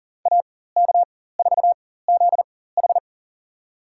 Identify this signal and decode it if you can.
Morse code, heard as a keyed tone, about 40 wpm, spaced as for 19 wpm, 700 Hz. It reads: AK4ZH
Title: USA Call Signs - 5 Characters - 4x Character Spacing - 40wpm